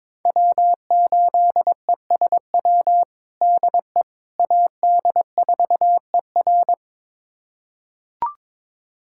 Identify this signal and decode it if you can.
Morse code, heard as a keyed tone, about 21 wpm, 700 Hz. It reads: W8ESW DE AD4ER
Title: USA Callsign DE Callsign - 5 Characters - 22wpm